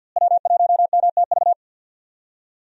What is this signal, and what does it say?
Morse code, heard as a keyed tone, about 49 wpm, 700 Hz. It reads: W1MTV
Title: USA Call Signs - 5 Characters - Repeated 3x - 50wpm